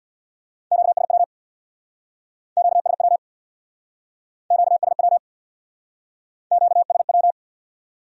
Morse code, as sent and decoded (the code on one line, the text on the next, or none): --.- ... .-- / --.- ... .-- / --.- ... .-- / --.- ... .--
QSW QSW QSW QSW